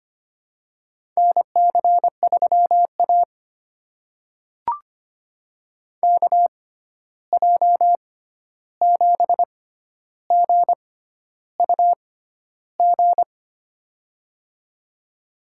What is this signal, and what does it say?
Morse code, heard as a keyed tone, about 25 wpm, spaced as for 11 wpm, 700 Hz. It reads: NC3A KJ7GUG
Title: USA Call Signs - 6x Character Spacing - 25wpm